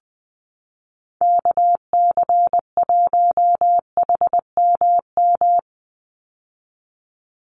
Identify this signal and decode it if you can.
Morse code, heard as a keyed tone, about 20 wpm, 700 Hz. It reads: KC1HMM
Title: USA Call Signs 20wpm